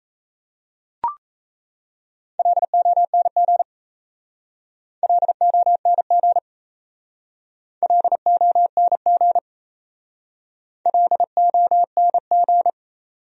Morse code, heard as a keyed tone, about 34 wpm, 700 Hz. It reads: LONG LONG LONG LONG
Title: Speed-Racing Top 100 Words 28wpm